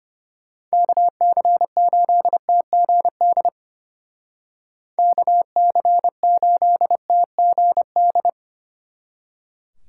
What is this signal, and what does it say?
Morse code, heard as a keyed tone, about 30 wpm, 700 Hz. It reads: KC8TGD KC8TGD
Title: Speed Racing - USA 2x3 Call Signs 25wpm